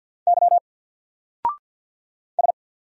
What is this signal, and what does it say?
Morse code, heard as a keyed tone, about 48 wpm, 700 Hz. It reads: Y S
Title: Single Letter-Number 50wpm